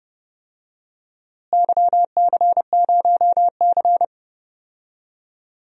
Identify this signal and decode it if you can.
Morse code, heard as a keyed tone, about 31 wpm, 700 Hz. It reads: YC0C
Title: DXCC-100 Call Signs - 4 Characters - Repeated 2x - 30wpm